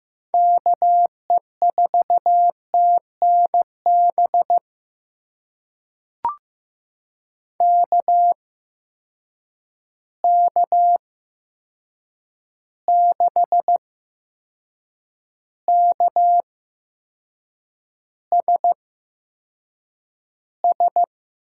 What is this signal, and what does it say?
Morse code, heard as a keyed tone, about 15 wpm, spaced as for 6 wpm, 700 Hz. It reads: KE4TNB KK6KSS
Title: USA Call Signs - 6 Characters - 8x Character Spacing - 15wpm